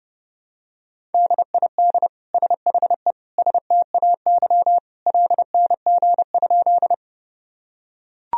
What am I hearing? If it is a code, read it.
Morse code, heard as a keyed tone, about 30 wpm, 700 Hz. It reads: DID SHE STAY LNG?
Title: Sentences from Top 500 Words and Abbreviations 30wpm